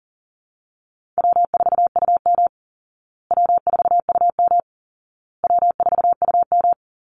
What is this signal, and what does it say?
Morse code, heard as a keyed tone, about 37 wpm, 700 Hz. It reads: W4UM W4UM W4UM
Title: Easy Call Signs Repeated 3x - 40wpm